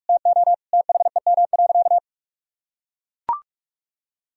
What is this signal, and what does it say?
Morse code, heard as a keyed tone, about 42 wpm, 700 Hz. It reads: TO THEM.
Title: Sentences from Top 1000 Words - Review Part 4 - 45wpm